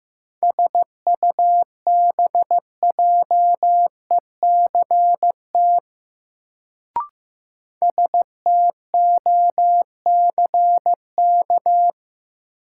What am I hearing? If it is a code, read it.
Morse code, heard as a keyed tone, about 15 wpm, 700 Hz. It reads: SUBJECT STOCK
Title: Sentences from Top 800 Words - New Word Drill - 15wpm